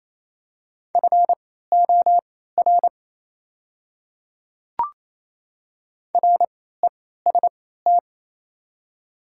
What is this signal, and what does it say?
Morse code, heard as a keyed tone, about 28 wpm, spaced as for 16 wpm, 700 Hz. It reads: FOR REST
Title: 3x Character Spacing - Top 500 Words - 28wpm